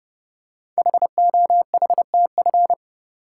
Morse code, heard as a keyed tone, about 30 wpm, 700 Hz. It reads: HOHTF
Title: Code Groups - 5 Letters - 30wpm